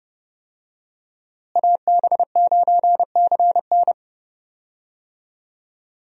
Morse code, heard as a keyed tone, about 30 wpm, 700 Hz. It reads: AB9CN
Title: USA Call Signs - 5 Characters - Repeated 3x - 30wpm